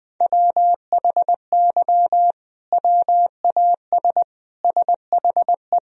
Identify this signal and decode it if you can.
Morse code, heard as a keyed tone, about 19 wpm, 700 Hz. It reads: WHY WAS SHE
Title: Sentences from Top 1000 Words - Review Part 3 - 20wpm